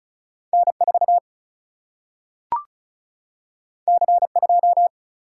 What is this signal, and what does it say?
Morse code, heard as a keyed tone, about 36 wpm, 700 Hz. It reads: N4 C2
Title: Single-Letter Single-Number 35wpm